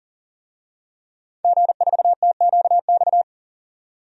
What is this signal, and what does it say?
Morse code, heard as a keyed tone, about 40 wpm, 700 Hz. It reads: G4TQX
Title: DXCC-100 Call Signs - 5 Characters - 40wpm